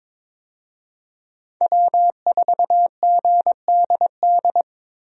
Morse code, heard as a keyed tone, about 22 wpm, 700 Hz. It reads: W4GDD
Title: USA Call Signs - 5 Characters - 22wpm